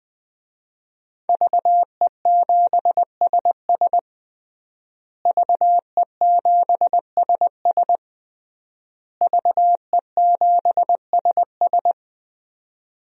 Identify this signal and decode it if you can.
Morse code, heard as a keyed tone, about 20 wpm, 700 Hz. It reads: VE7SS VE7SS VE7SS
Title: DXCC-100 Call Signs - 5 Characters - Repeated 3x - 20wpm